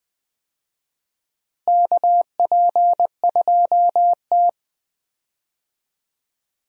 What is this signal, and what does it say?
Morse code, heard as a keyed tone, about 21 wpm, 700 Hz. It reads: KP2T
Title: USA Call Signs - 4 Characters - Repeated 3x - 20wpm